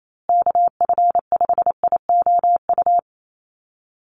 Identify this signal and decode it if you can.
Morse code, heard as a keyed tone, about 28 wpm, 700 Hz. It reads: KF5IOU
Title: USA Call Signs Repeated 2x 28wpm